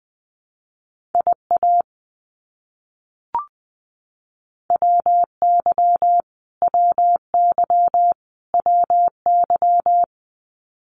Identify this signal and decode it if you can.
Morse code, heard as a keyed tone, about 20 wpm, 700 Hz. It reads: IA WY WY WY
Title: US States Repeated 3x 20wpm